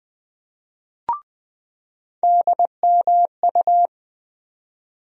Morse code, heard as a keyed tone, about 20 wpm, 700 Hz. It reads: DMU
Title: USA Call Sign Postfixes - 3 Letters 20wpm